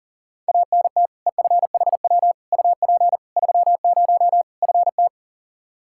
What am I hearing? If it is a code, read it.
Morse code, heard as a keyed tone, about 40 wpm, 700 Hz. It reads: ANT EFHW UP 30 FT
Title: Practice QSO - Exchange 2 - Info - 40wpm